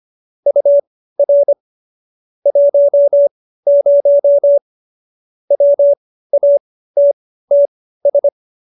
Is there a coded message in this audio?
Yes: Morse code from 0.5 to 8.3 s, heard as a keyed tone, about 25 wpm, spaced as for 15 wpm, 570 Hz. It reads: UR 10 WATTS